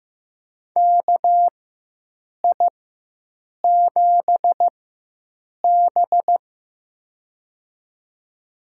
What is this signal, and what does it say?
Morse code, heard as a keyed tone, about 15 wpm, spaced as for 7 wpm, 700 Hz. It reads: KI7B
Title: USA Call Signs - 4 Characters - 4x Character Spacing - 15wpm